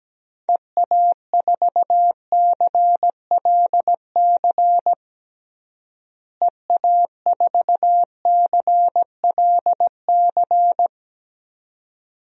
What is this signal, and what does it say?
Morse code, heard as a keyed tone, about 17 wpm, 700 Hz. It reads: EA4CLC EA4CLC
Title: DXCC-100 Call Signs - 6 Characters - Repeated 2x - 17wpm